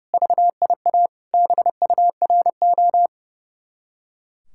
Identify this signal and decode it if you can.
Morse code, heard as a keyed tone, about 36 wpm, 700 Hz. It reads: VIA BURO
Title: Speed-Racing Common QSO Phrases - 30wpm